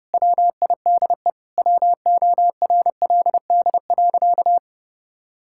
Morse code, heard as a keyed tone, about 30 wpm, 700 Hz. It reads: WIDE WORLD.
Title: Sentences from Top 1000 Words - Review Part 6 - 30wpm